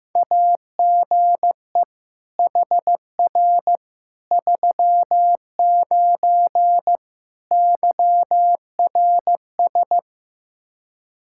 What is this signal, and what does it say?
Morse code, heard as a keyed tone, about 16 wpm, 700 Hz. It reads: AGE HR 39 YRS
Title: Practice QSO - Exchange 3 - More Info - 15wpm